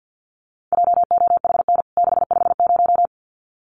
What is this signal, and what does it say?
Morse code, heard as a keyed tone, about 50 wpm, 700 Hz. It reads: JOHN 650
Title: CWT Exchange 50wpm